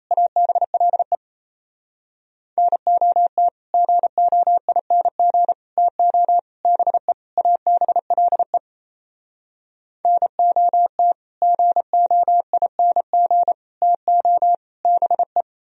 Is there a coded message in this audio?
Yes: Morse code from 0.1 to 15.4 s, heard as a keyed tone, about 37 wpm, 700 Hz. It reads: ABLE NOT GOING TO BE ABLE NOT GOING TO BE